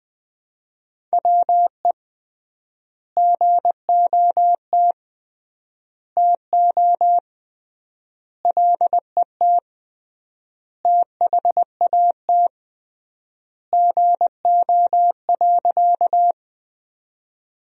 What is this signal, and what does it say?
Morse code, heard as a keyed tone, about 20 wpm, 700 Hz. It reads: WE GOT TO LET THAT GO.
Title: 3x Word-Spacing - Sentences from Top 300 Words - 20wpm